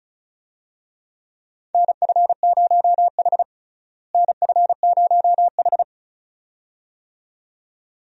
Morse code, heard as a keyed tone, about 36 wpm, 700 Hz. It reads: NF0H NF0H
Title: USA Call Signs - 4 Characters - Repeated 2x - 35wpm